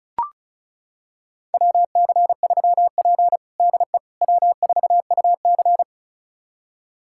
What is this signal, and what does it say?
Morse code, heard as a keyed tone, about 34 wpm, 700 Hz. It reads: WC3P DE W4UC
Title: USA Callsign DE Callsign - 4 Characters - 35wpm